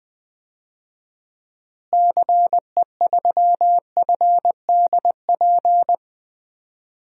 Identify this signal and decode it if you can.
Morse code, heard as a keyed tone, about 20 wpm, 700 Hz. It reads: CE3FDP
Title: DXCC-100 Call Signs - 6 Characters - Repeated 2x - 20wpm